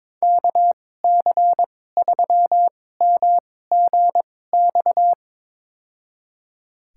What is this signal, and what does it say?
Morse code, heard as a keyed tone, about 22 wpm, spaced as for 16 wpm, 700 Hz. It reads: KC3MGX
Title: USA Call Signs - 6 Characters - 2x Character Spacing - 22wpm